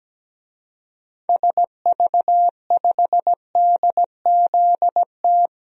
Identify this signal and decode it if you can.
Morse code, heard as a keyed tone, about 17 wpm, 700 Hz. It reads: SV5DZT
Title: DXCC-100 Call Signs - 6 Characters - 17wpm